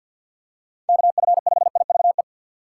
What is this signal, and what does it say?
Morse code, heard as a keyed tone, about 47 wpm, 700 Hz. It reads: KF5IVE
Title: USA Call Signs - 3x Character Spacing - 50wpm